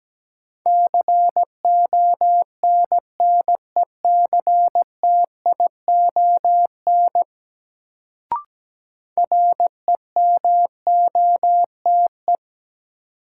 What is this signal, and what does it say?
Morse code, heard as a keyed tone, about 17 wpm, 700 Hz. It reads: CONNECTION REMOTE
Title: Sentences from Top 1300 Words - New Word Drill - 17wpm